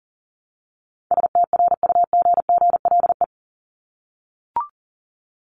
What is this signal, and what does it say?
Morse code, heard as a keyed tone, about 40 wpm, 700 Hz. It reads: STRUGGLE
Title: Top 1751-2000 Words 40wpm